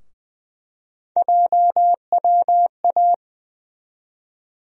Morse code, heard as a keyed tone, about 20 wpm, 700 Hz. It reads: JWA
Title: USA Call Sign Postfixes - 3 Letters 20wpm